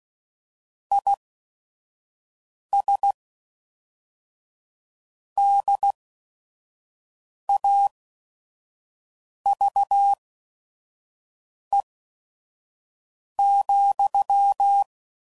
Morse code, heard as a keyed tone, about 16 wpm, spaced as for 5 wpm, 800 Hz. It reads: IS DAVE,
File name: CW Code Groups @5WPM / QSOs / kc7haa